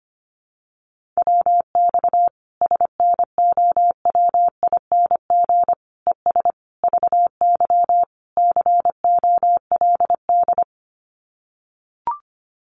Morse code, heard as a keyed tone, about 25 wpm, 700 Hz. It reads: WX SNOWING ES VY COLD
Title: Common QSO Phrases 25wpm